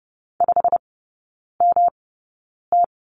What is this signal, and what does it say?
Morse code, heard as a keyed tone, about 30 wpm, spaced as for 9 wpm, 700 Hz. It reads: 5MT
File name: Lesson 18e - Letter m - Three Characters 30wpm